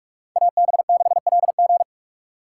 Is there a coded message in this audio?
Yes: Morse code from 0.4 to 1.8 s, heard as a keyed tone, about 44 wpm, 700 Hz. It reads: AB6LG